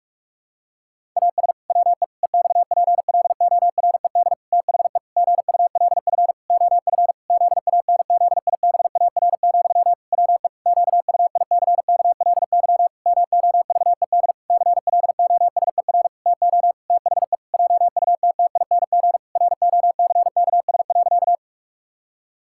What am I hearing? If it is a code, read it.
Morse code, heard as a keyed tone, about 43 wpm, 700 Hz. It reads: AS WE EXPLORED THE GULF OF ZANZIBAR, WE QUICKLY MOVED CLOSER TO THE JUTTING ROCKS.
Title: Pangram Challenge v2 - 45wpm